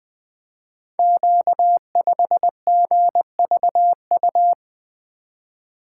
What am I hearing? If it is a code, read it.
Morse code, heard as a keyed tone, about 20 wpm, 700 Hz. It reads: Q5GVU
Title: Code Groups - 5 Letter:Numbers - 20wpm